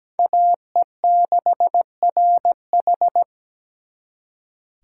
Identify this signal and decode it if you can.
Morse code, heard as a keyed tone, about 20 wpm, 700 Hz. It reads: AE6RH
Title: Speed Racing - USA 2x2 Call Signs 17wpm